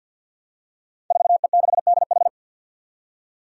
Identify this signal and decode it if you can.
Morse code, heard as a keyed tone, about 52 wpm, 700 Hz. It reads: 4E6DH
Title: Five Challenging Letter-Number Combinations 50wpm